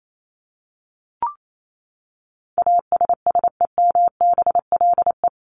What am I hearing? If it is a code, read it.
Morse code, heard as a keyed tone, about 28 wpm, 700 Hz. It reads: ASSEMBLE
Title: LY Suffix - Top 100 Words with Base Words 28wpm